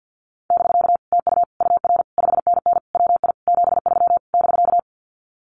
Morse code, heard as a keyed tone, about 50 wpm, 700 Hz. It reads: <BK> TU UR 5NN WI 73 <BK>
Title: Common QSO Phrases Encore 50wpm